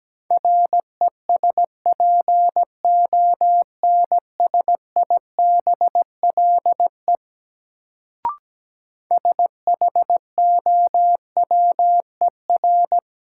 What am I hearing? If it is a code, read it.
Morse code, heard as a keyed tone, about 17 wpm, 700 Hz. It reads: RESPONSIBLE SHOWER
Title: Sentences from Top 1600 Words - New Word Drill Alt - 17wpm